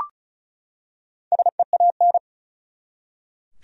Morse code, heard as a keyed tone, about 36 wpm, 700 Hz. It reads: SEAN
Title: USA Common Names - Max 5 Letters - 35wpm